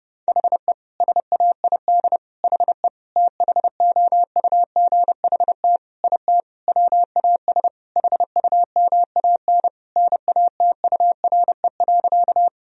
Morse code, heard as a keyed tone, about 30 wpm, 700 Hz. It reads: HE SAID HE THOUGHT IT WAS HUMAN NATURE.
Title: Sentences from Top 1000 Words - Review Part 3 - 30wpm